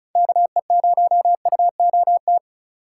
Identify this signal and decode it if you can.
Morse code, heard as a keyed tone, about 36 wpm, 700 Hz. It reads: KE0UOT
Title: USA Call Signs - 6 Characters - 2x Character Spacing - 35wpm